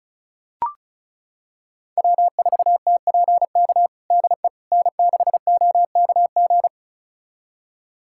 Morse code, heard as a keyed tone, about 36 wpm, 700 Hz. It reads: W4TPK DE N6OKG
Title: USA Callsign DE Callsign - 5 Characters - 35wpm